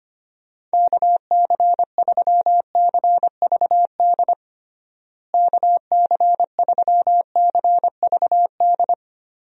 Morse code, heard as a keyed tone, about 25 wpm, 700 Hz. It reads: KC3CVD KC3CVD
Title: USA Call Signs - 6 Characters - Repeated 3x - 25wpm